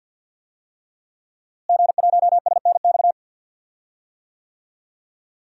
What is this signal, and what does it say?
Morse code, heard as a keyed tone, about 52 wpm, 700 Hz. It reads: G1SNX
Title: DXCC-100 Call Signs - 5 Characters - Repeated 3x - 50wpm